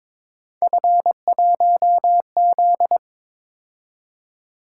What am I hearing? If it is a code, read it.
Morse code, heard as a keyed tone, about 23 wpm, 700 Hz. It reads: F1Z
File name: practice / 22wpm / Three Letter-Number Combinations - 22wpm